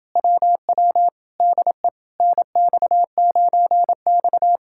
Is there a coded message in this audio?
Yes: Morse code from 0.2 to 4.6 s, heard as a keyed tone, about 29 wpm, 700 Hz. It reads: WW DE NX9X